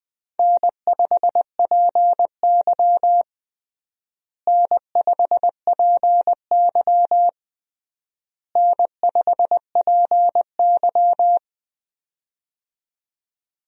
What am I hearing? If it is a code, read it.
Morse code, heard as a keyed tone, about 21 wpm, 700 Hz. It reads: N5PY N5PY N5PY
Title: USA Call Signs - 4 Characters - Repeated 3x - 20wpm